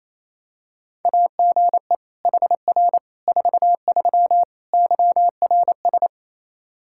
Morse code, heard as a keyed tone, about 28 wpm, 700 Hz. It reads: AGE HR 43 YRS
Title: K1USN SST QSO Practice 28wpm